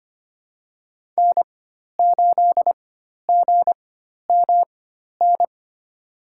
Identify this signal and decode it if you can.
Morse code, heard as a keyed tone, about 25 wpm, spaced as for 12 wpm, 700 Hz. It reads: N8GMN
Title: USA Call Signs - 5 Characters - 4x Character Spacing - 25wpm